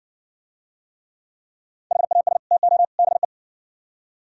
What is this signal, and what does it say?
Morse code, heard as a keyed tone, about 58 wpm, 700 Hz. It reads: HAS TO BE